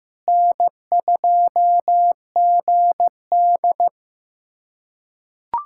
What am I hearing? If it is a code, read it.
Morse code, heard as a keyed tone, about 15 wpm, 700 Hz. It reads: N2GD
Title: USA Callsign DE Callsign - 4 Characters - 15wpm